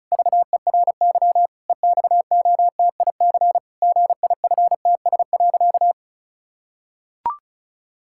Morse code, heard as a keyed tone, about 36 wpm, 700 Hz. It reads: VERY EXOTIC GIFTS.
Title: Pangram Challenge v2 - 35wpm